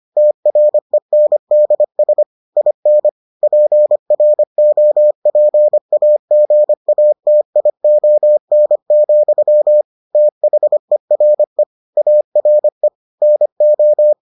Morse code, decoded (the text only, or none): TRENDS IN PROPAGATION, THERE ARE NO